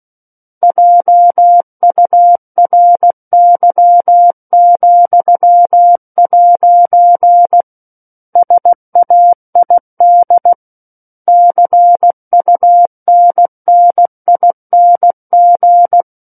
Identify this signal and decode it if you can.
Morse code, heard as a keyed tone, about 16 wpm, 700 Hz. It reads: JURY,' SAID CUNNING